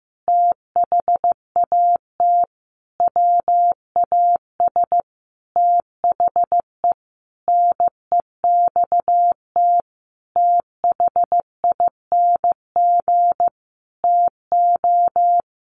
Morse code, heard as a keyed tone, about 15 wpm, 700 Hz. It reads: THAT WAS THE NEXT THING TO
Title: Sentences from Top 1300 Words 15wpm